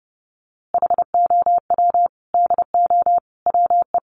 Morse code, heard as a keyed tone, about 30 wpm, 700 Hz. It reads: HOW DO WE
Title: A Sound Of Thunder 30wpm